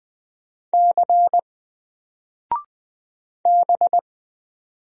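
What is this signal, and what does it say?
Morse code, heard as a keyed tone, about 20 wpm, 700 Hz. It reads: C B
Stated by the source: Single letter and number Morse drill at 20 wpm, from N4LCD.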